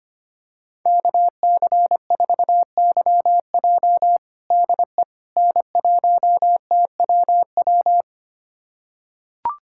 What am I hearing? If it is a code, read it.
Morse code, heard as a keyed tone, about 25 wpm, 700 Hz. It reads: KC4YJ DE N1TWW
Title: USA Callsign DE Callsign - 5 Characters - 25wpm